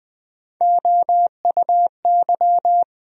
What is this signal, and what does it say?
Morse code, heard as a keyed tone, about 20 wpm, 700 Hz. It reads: OUY
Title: USA Call Sign Postfixes - 3 Letters 20wpm